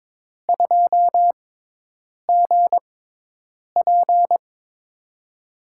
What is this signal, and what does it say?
Morse code, heard as a keyed tone, about 22 wpm, spaced as for 8 wpm, 700 Hz. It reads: 2GP
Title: USA Call Signs - 5 Characters - 6x Character Spacing - 22wpm